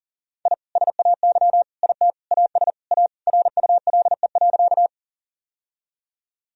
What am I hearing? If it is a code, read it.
Morse code, heard as a keyed tone, about 40 wpm, 700 Hz. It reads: I SAY IT AS A RULE.
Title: Sentences from Top 1000 Words - Review Part 2 - 40wpm